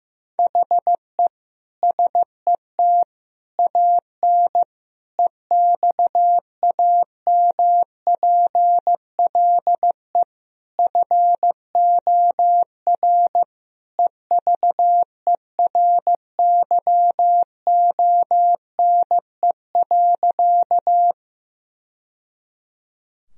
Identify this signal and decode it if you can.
Morse code, heard as a keyed tone, about 16 wpm, 700 Hz. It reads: HE SET AN EXAMPLE FOR EVERYONE.
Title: Sentences from Top 1000 Words - Review Part 1 - 15wpm